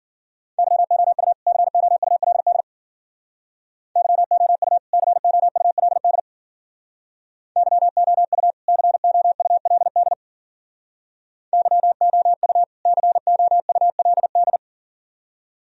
Morse code, heard as a keyed tone, about 52 wpm, 700 Hz. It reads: YOU COULD YOU COULD YOU COULD YOU COULD